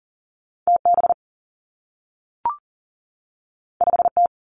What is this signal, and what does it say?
Morse code, heard as a keyed tone, about 42 wpm, 700 Hz. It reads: TB 5T
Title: Two Challenging Letter-Number Combinations 40wpm